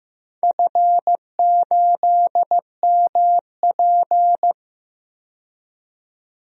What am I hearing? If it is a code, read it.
Morse code, heard as a keyed tone, about 15 wpm, 700 Hz. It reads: F8MP
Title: DXCC-100 Call Signs - 4 Characters - 15wpm